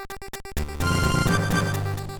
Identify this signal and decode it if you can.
There is no code.